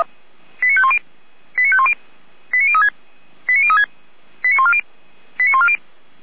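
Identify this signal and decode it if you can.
Five-tone selective call (ZVEI), heard as quick runs of five tones, tones of about 70 ms. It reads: 89510 89510 89036 89036 89140 89140
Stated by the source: Five-tone ZVEI selective call 89070 received by SDR.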